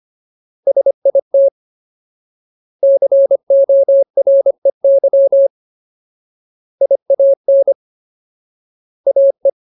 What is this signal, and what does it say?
Morse code, heard as a keyed tone, about 25 wpm, 550 Hz. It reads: SIT COREY IAN AE
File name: Ultimate Word List / 25WPM Ultimate